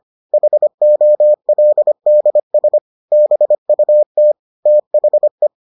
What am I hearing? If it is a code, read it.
Morse code, heard as a keyed tone, about 25 wpm, 600 Hz. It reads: HOLDS BUT THE